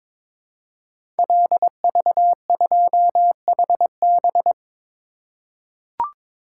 Morse code, heard as a keyed tone, about 22 wpm, 700 Hz. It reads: LV2HB
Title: Code Groups - 5 Letter:Numbers - 22wpm